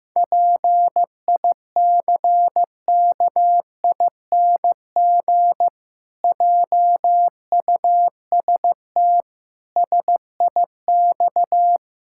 Morse code, heard as a keyed tone, about 15 wpm, 700 Hz. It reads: PICKING JUST SIX
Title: Pangram Challenge v2 - 15wpm